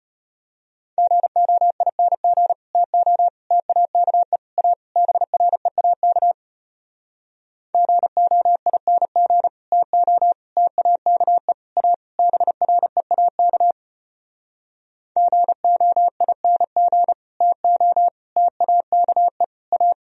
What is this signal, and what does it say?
Morse code, heard as a keyed tone, about 33 wpm, 700 Hz. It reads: GOING TO TAKE A BREAK GOING TO TAKE A BREAK GOING TO TAKE A